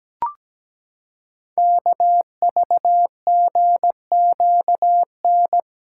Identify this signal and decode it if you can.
Morse code, heard as a keyed tone, about 17 wpm, 700 Hz. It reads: KVGQN
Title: Code Groups - 5 Letters - 17wpm